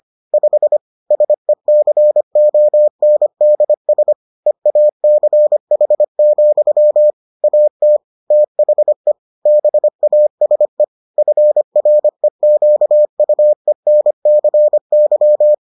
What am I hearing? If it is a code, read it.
Morse code, heard as a keyed tone, about 25 wpm, 600 Hz. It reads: 5 SECONDS EACH, AT THE BASE FREQUENCY